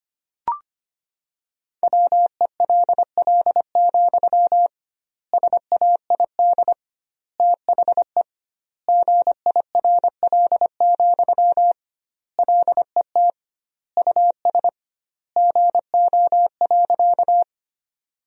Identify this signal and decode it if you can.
Morse code, heard as a keyed tone, about 25 wpm, 700 Hz. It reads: WELL, SAID THE GIRL, LET US GO.
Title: 2x Word-Spacing - Sentences from Top 400 Words - 25wpm